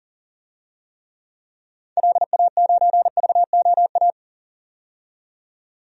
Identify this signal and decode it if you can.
Morse code, heard as a keyed tone, about 40 wpm, 700 Hz. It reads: LA9VOA
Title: DXCC-100 Call Signs - 6 Characters - 40wpm